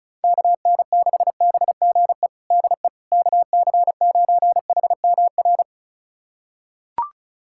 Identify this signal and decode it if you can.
Morse code, heard as a keyed tone, about 36 wpm, 700 Hz. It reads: KN6BGE DE KC9HMR
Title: USA Callsign DE Callsign 35wpm